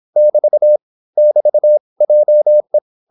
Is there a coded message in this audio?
Yes: Morse code from 0.2 to 2.8 s, heard as a keyed tone, about 26 wpm, 600 Hz. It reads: <BT> <BT> JE